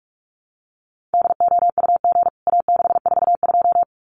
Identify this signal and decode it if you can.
Morse code, heard as a keyed tone, about 44 wpm, 700 Hz. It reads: DOUG A642
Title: CWT Exchange 45wpm